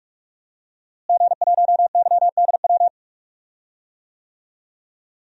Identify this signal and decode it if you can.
Morse code, heard as a keyed tone, about 44 wpm, 700 Hz. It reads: G1YDW
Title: DXCC-100 Call Signs - 5 Characters - Repeated 2x - 45wpm